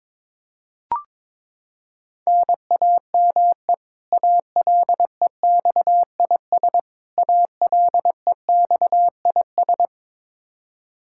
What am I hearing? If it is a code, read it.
Morse code, heard as a keyed tone, about 22 wpm, 700 Hz. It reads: NAME ALEXIS ALEXIS
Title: Practice QSO - Exchange 1 - Beginning - 22wpm